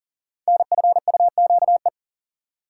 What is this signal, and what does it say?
Morse code, heard as a keyed tone, about 38 wpm, 700 Hz. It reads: NFUQE
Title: Code Groups - 5 Letter:Numbers - 40wpm